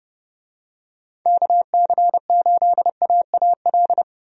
Morse code, heard as a keyed tone, about 30 wpm, 700 Hz. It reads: KC8AAL
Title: USA Call Signs - 6 Characters - 30wpm